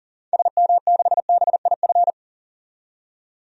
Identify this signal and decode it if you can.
Morse code, heard as a keyed tone, about 40 wpm, 700 Hz. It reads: SM6BIF
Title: DXCC-100 Call Signs - 6 Characters - 40wpm